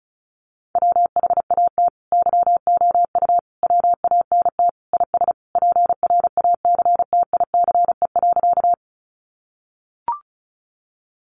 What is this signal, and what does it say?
Morse code, heard as a keyed tone, about 34 wpm, 700 Hz. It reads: WHAT YOU WANT IS PRACTICE.
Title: Sentences from Top 700 Words 35wpm